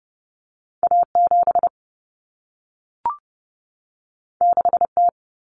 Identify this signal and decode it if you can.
Morse code, heard as a keyed tone, about 30 wpm, 700 Hz. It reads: A7 6T
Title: Common Cut Numbers - 2-Digit 30wpm